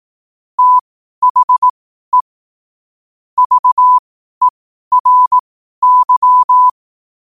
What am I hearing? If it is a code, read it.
Morse code, heard as a keyed tone, about 18 wpm, spaced as for 12 wpm, 1000 Hz. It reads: THE VERY
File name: qotdfiles / 12wpm